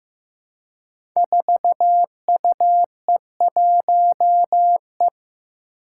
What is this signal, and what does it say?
Morse code, heard as a keyed tone, about 15 wpm, 700 Hz. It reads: 4UE1E
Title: Five Challenging Letter-Number Combinations 15wpm